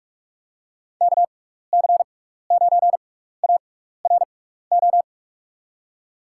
Morse code, heard as a keyed tone, about 46 wpm, spaced as for 15 wpm, 700 Hz. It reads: KC9ARO
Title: USA Call Signs - 6 Characters - 6x Character Spacing - 45wpm